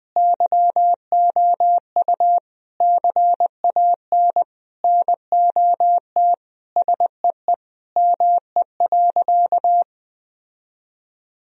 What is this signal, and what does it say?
Morse code, heard as a keyed tone, about 21 wpm, 700 Hz. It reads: YOU CAN NOT SEE ME.
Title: Sentences from Top 1000 Words - Review Part 3 - 20wpm